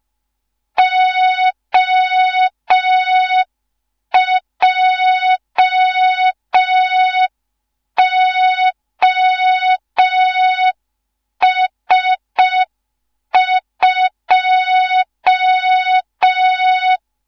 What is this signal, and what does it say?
Morse code, heard as a keyed tone, about 5 wpm, 3700 Hz. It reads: OJOS2
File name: Morze-HA2MN / hang / 15 GYAKORLAT